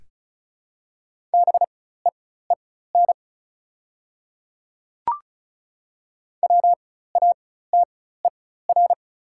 Morse code, heard as a keyed tone, about 35 wpm, spaced as for 16 wpm, 690 Hz. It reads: BEEN WATER
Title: 4x Character Spacing - Top 500 Words - 35wpm